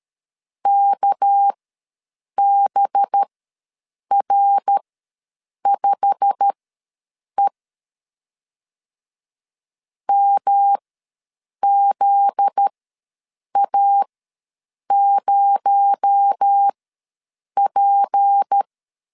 Morse code, heard as a keyed tone, about 13 wpm, spaced as for 7 wpm, 800 Hz. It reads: KBR5E MZA0P
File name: LEZIONI / 8 YJ45 / 8 YJ45 ws 12wpm 3146